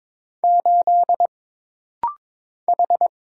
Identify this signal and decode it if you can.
Morse code, heard as a keyed tone, about 22 wpm, 700 Hz. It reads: 8 H
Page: This rapid-fire timing is geared to push your ICR (Instant Character Recognition) skills.